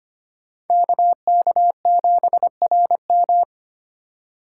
Morse code, heard as a keyed tone, about 25 wpm, 700 Hz. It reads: KK7RM
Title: USA Call Signs - 5 Characters - 4x Character Spacing - 25wpm